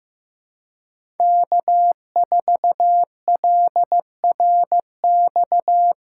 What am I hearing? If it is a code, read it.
Morse code, heard as a keyed tone, about 15 wpm, 700 Hz. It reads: K4LRX
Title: K1USN SST QSO Practice - 15wpm